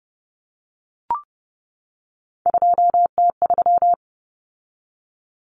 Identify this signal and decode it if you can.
Morse code, heard as a keyed tone, about 30 wpm, 700 Hz. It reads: 2T3
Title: Common Cut Numbers - 3-Digit 30wpm